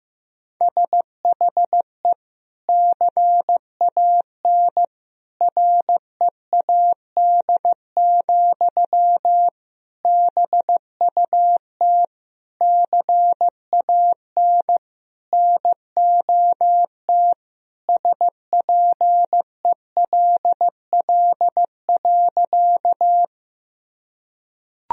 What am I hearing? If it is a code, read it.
Morse code, heard as a keyed tone, about 15 wpm, 700 Hz. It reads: SHE CAN READ, BUT CAN NOT SPELL.
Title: Sentences from Top 1000 Words - Review Part 1 - 15wpm